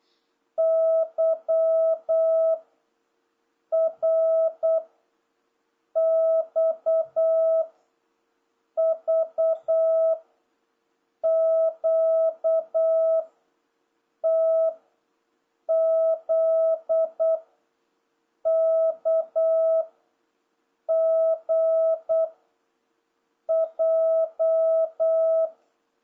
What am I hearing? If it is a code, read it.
Morse code, heard as a keyed tone, about 8 wpm, 650 Hz. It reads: Y R X V Q T Z K G J